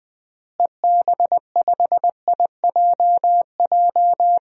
Morse code, heard as a keyed tone, about 20 wpm, 700 Hz. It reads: EB5IJJ
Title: DXCC-100 Call Signs - 6 Characters - Repeated 2x - 20wpm